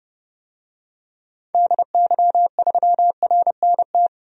Morse code, heard as a keyed tone, about 30 wpm, 700 Hz. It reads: DY3RNT
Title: DXCC-100 Call Signs - 6 Characters - 30wpm